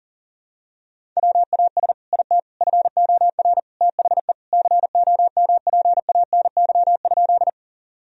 Morse code, heard as a keyed tone, about 40 wpm, 700 Hz. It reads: WAS IT FOR THE COMPANY?
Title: Sentences from Top 1000 Words - Review Part 3 - 40wpm